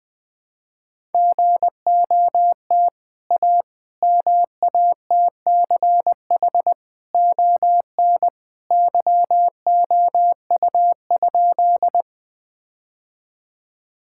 Morse code, heard as a keyed tone, about 20 wpm, 700 Hz. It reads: GOT A MATCH ON YOU?
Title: Sentences from Top 1600 Words 20wpm